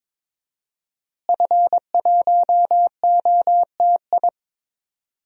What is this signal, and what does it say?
Morse code, heard as a keyed tone, about 22 wpm, 700 Hz. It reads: F1OTI
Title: DXCC-100 Call Signs - 5 Characters - Repeated 2x - 22wpm